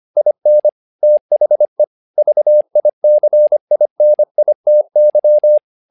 Morse code, heard as a keyed tone, about 25 wpm, 600 Hz. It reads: IN THE VICINITY
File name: learning center / tower trouble / tower trouble 25wpm